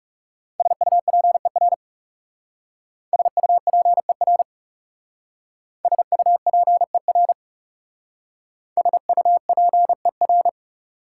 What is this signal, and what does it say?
Morse code, heard as a keyed tone, about 38 wpm, 690 Hz. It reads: SUPER SUPER SUPER SUPER